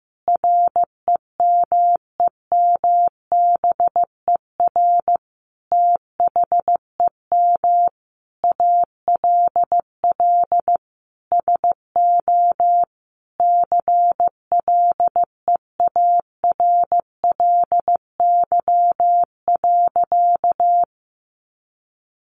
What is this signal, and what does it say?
Morse code, heard as a keyed tone, about 15 wpm, 700 Hz. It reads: REMEMBER THEM ALL SO CLEARLY.
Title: Sentences from Top 1300 Words 15wpm